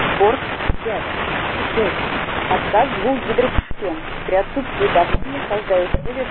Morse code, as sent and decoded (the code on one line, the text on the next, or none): none